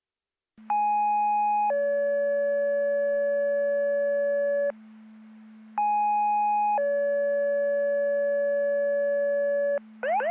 Two-tone page, heard as a one-second tone, then a three-second tone, about 830 then 570 Hz, about 830 then 570 Hz, with noise. It is heard from 0.7 to 9.8 s.